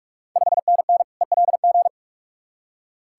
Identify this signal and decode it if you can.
Morse code, heard as a keyed tone, about 44 wpm, 690 Hz. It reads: 5NN ELG
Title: Ontario QSO Party Exchange 45wpm